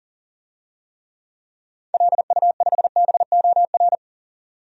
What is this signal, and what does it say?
Morse code, heard as a keyed tone, about 40 wpm, 700 Hz. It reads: LU5BOR
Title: DXCC-100 Call Signs - 6 Characters - 40wpm